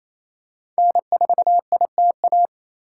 Morse code, heard as a keyed tone, about 28 wpm, 700 Hz. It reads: N4ITA
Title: USA Call Signs - 5 Characters - 4x Character Spacing - 28wpm